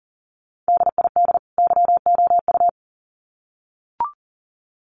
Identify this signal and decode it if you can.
Morse code, heard as a keyed tone, about 40 wpm, 700 Hz. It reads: DID YOU
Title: Sets of 2 Words 40wpm